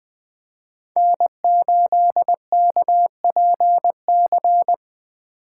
Morse code, heard as a keyed tone, about 20 wpm, 700 Hz. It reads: N8KPC
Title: Speed Racing - USA Call Signs 17wpm